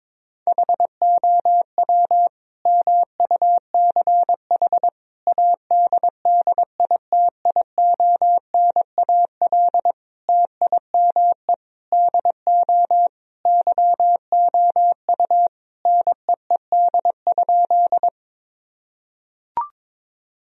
Morse code, heard as a keyed tone, about 22 wpm, 700 Hz. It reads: HOW MUCH ADDITIONAL TIME DO YOU NEED?
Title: Sentences from Top 1000 Words - Review Part 4 - 22wpm